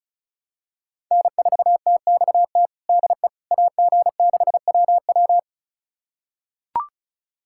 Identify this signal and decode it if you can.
Morse code, heard as a keyed tone, about 34 wpm, 690 Hz. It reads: N4TXT DE AG6WW
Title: USA Callsign DE Callsign - 5 Characters - 35wpm